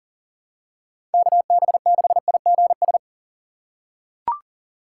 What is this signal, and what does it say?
Morse code, heard as a keyed tone, about 37 wpm, 700 Hz. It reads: KB6IGS
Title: USA Call Signs - 6x Character Spacing - 40wpm